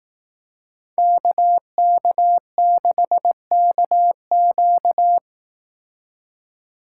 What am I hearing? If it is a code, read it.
Morse code, heard as a keyed tone, about 18 wpm, 700 Hz. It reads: KK6KQ
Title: Speed Racing - USA 2x2 Call Signs 15wpm